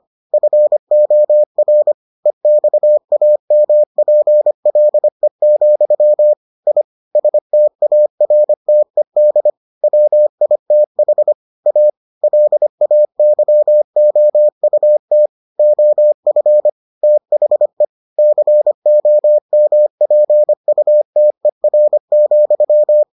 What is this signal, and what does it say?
Morse code, heard as a keyed tone, about 25 wpm, 600 Hz. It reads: FOR EXAMPLE, I STARTED WITH A LAYOUT OF THE COMPUTER,